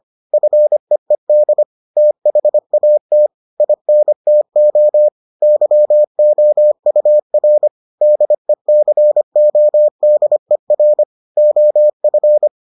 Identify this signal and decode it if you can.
Morse code, heard as a keyed tone, about 25 wpm, 600 Hz. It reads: FEED THAT INTO YOUR DECODER OF